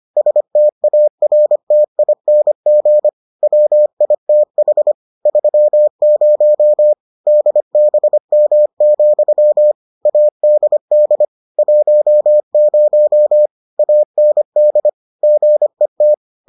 Morse code, heard as a keyed tone, about 25 wpm, 600 Hz. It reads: STARTING WITH 30 DBM, ADD 10 AND GET